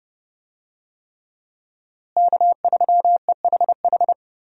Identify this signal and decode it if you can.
Morse code, heard as a keyed tone, about 30 wpm, 700 Hz. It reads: K3EHH
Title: USA Call Signs - 5 Characters - Repeated 3x - 30wpm